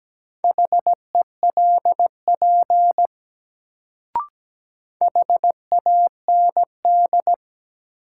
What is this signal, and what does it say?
Morse code, heard as a keyed tone, about 17 wpm, 700 Hz. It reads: HELP HAND
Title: Sentences from Top 200 Words - New Word Drill - 17wpm